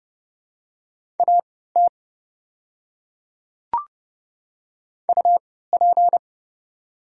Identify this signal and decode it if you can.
Morse code, heard as a keyed tone, about 30 wpm, spaced as for 17 wpm, 700 Hz. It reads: AT UP
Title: 3x Character Spacing - Top 100 Words - 30wpm